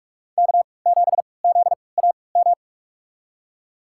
Morse code, heard as a keyed tone, about 46 wpm, spaced as for 25 wpm, 700 Hz. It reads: K7ZAM
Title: USA Call Signs - 5 Characters - 3x Character Spacing - 45wpm